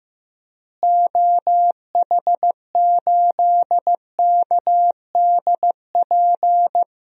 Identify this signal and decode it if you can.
Morse code, heard as a keyed tone, about 15 wpm, 700 Hz. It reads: OH8KDP
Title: DXCC-100 Call Signs - 6 Characters - Repeated 2x - 15wpm